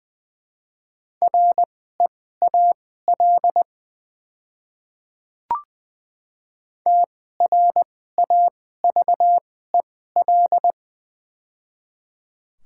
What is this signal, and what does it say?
Morse code, heard as a keyed tone, about 20 wpm, spaced as for 14 wpm, 700 Hz. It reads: REAL TRAVEL
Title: 2x Character Spacing - Top 500 Words - 20wpm